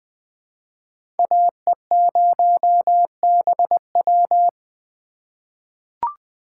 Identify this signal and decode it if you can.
Morse code, heard as a keyed tone, about 20 wpm, 700 Hz. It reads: AE0BW
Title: USA Call Signs - 6x Character Spacing - 20wpm